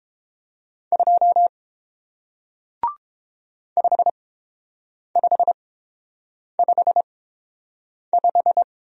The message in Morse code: 2 5555

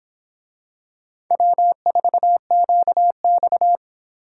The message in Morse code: W4QX

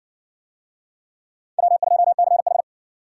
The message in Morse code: W2CH

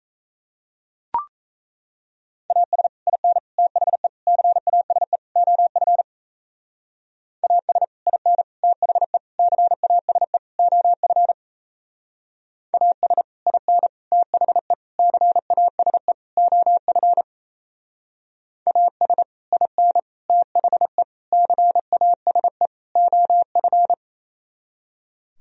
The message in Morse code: AS IN THE CASE OF AS IN THE CASE OF AS IN THE CASE OF AS IN THE CASE OF

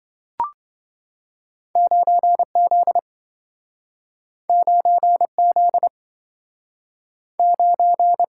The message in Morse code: 9Z 9Z 9

